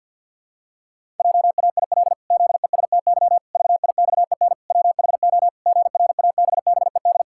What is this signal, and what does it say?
Morse code, heard as a keyed tone, about 53 wpm, 700 Hz. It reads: JAIL ZESTY VIXEN WHO GRABBED
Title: Pangram Challenge v2 - 50wpm